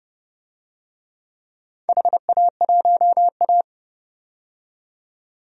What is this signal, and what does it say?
Morse code, heard as a keyed tone, about 30 wpm, 700 Hz. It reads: HA1A
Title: DXCC-100 Call Signs - 4 Characters - 30wpm